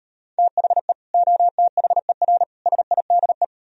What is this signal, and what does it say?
Morse code, heard as a keyed tone, about 37 wpm, 700 Hz. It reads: THE OTHER SIDE